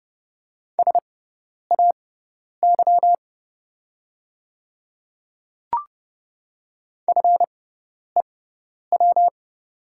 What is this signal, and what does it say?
Morse code, heard as a keyed tone, about 30 wpm, spaced as for 10 wpm, 700 Hz. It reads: SAY FEW
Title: 6x Character Spacing - Top 300 Words - 30wpm